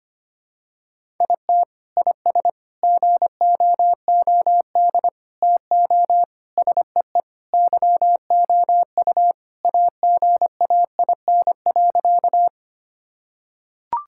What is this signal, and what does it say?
Morse code, heard as a keyed tone, about 25 wpm, 700 Hz. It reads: IT IS GOOD TO SEE YOU AGAIN.
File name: practice / 25wpm / Sentences with 4-Word Phrases - 25wpm